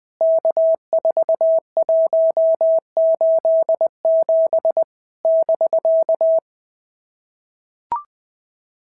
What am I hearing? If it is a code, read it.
Morse code, heard as a keyed tone, about 20 wpm, 650 Hz. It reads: K4187 <BK>
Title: POTA QSO Practice - 20wpm